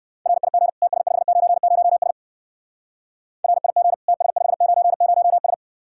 Morse code, heard as a keyed tone, about 59 wpm, 700 Hz. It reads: RIG TS590S RIG TS590S